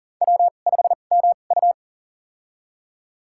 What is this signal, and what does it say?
Morse code, heard as a keyed tone, about 37 wpm, spaced as for 28 wpm, 700 Hz. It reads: W5MU
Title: USA Call Signs - 4 Characters - 2x Character Spacing - 40wpm